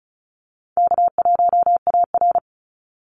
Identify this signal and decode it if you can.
Morse code, heard as a keyed tone, about 34 wpm, 700 Hz. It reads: K1AR
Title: Easy Call Signs Repeated 3x - 35wpm